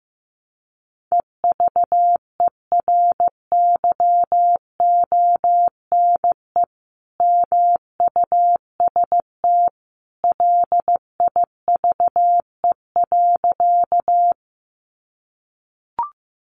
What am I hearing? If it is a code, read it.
Morse code, heard as a keyed tone, about 15 wpm, 700 Hz. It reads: EVERYONE MUST LIVE.
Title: Sentences from Top 800 Words 15wpm